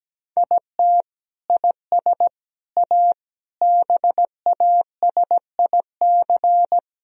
Morse code, heard as a keyed tone, about 17 wpm, 700 Hz. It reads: IT IS A BASIC
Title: Sentences from Top 1000 Words - Review Part 3 - 17wpm